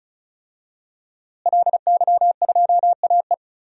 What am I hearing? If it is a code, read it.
Morse code, heard as a keyed tone, about 36 wpm, 700 Hz. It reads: LY2AE
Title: DXCC-100 Call Signs - 5 Characters - Repeated 2x - 35wpm